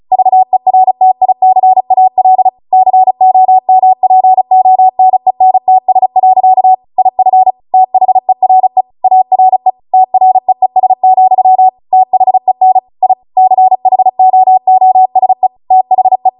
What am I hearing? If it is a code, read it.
Morse code, heard as a keyed tone, about 36 wpm, 750 Hz. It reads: VERTICAL COMPONENTS. IF THERE ARE TREES, THEN I CHOOSE THE
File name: Morse / 35WPM